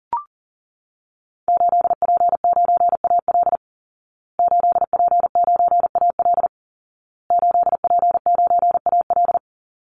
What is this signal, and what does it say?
Morse code, heard as a keyed tone, about 40 wpm, 700 Hz. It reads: 8P9AL 8P9AL 8P9AL
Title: Call Signs Repeated 2x - 40wpm